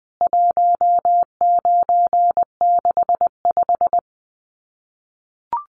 Morse code, heard as a keyed tone, about 20 wpm, 700 Hz. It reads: 1965
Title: CWT Exchange 20wpm